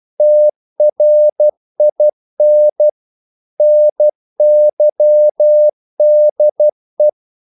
Morse code, heard as a keyed tone, about 12 wpm, 600 Hz. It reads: TRIN NYDE
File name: Korte ord DK 12wpm